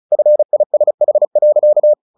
Morse code, heard as a keyed tone, about 35 wpm, 600 Hz. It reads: FISH.
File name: the old man and the sea / the old man and the sea 1 35wpm